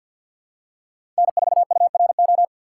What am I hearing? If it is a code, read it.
Morse code, heard as a keyed tone, about 49 wpm, 700 Hz. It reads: N4URO
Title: USA Call Signs - 5 Characters - 6x Character Spacing - 50wpm